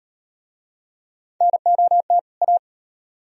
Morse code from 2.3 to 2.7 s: A

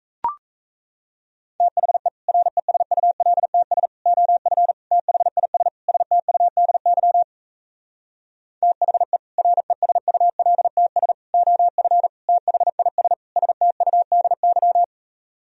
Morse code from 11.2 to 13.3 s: OF THIS

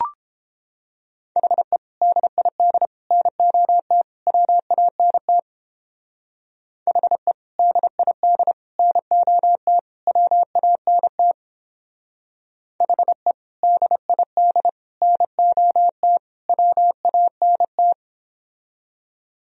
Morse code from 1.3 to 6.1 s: HE DID NOT WANT

